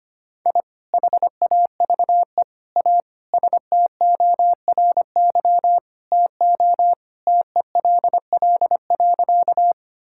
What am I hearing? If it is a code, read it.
Morse code, heard as a keyed tone, about 25 wpm, 700 Hz. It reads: I HAVE A STORY TO TELL.
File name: practice / 25wpm / Sentences with 2-Word Phrases - 25wpm